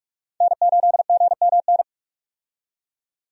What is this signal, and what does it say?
Morse code, heard as a keyed tone, about 42 wpm, 700 Hz. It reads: N8GMN
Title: USA Call Signs - 5 Characters - 45wpm